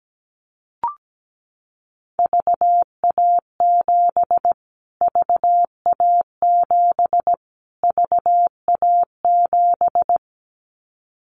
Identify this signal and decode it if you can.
Morse code, heard as a keyed tone, about 17 wpm, 700 Hz. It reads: VA7 VA7 VA7
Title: Callsign Prefixes for Top 25 DXCC - Repeated 3x - 17wpm